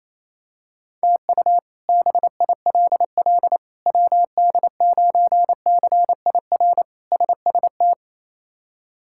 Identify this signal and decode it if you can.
Morse code, heard as a keyed tone, about 28 wpm, 700 Hz. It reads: TU BILL WD9CIR SST
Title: K1USN SST QSO Practice - 28wpm